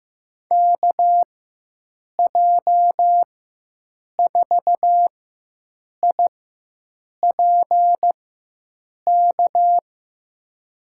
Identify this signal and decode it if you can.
Morse code, heard as a keyed tone, about 15 wpm, spaced as for 7 wpm, 700 Hz. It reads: KJ4IPK
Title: USA Call Signs - 6 Characters - 4x Character Spacing - 15wpm